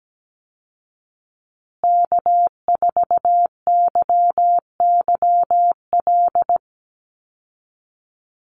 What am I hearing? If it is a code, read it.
Morse code, heard as a keyed tone, about 17 wpm, 700 Hz. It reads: K4YYL
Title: Call Signs Repeated 2x - 17wpm